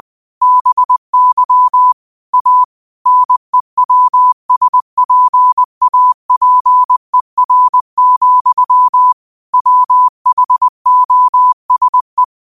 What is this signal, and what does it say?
Morse code, heard as a keyed tone, about 20 wpm, 1000 Hz. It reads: BY A NEWSPAPER, WHOSE
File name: qotdfiles / 20wpm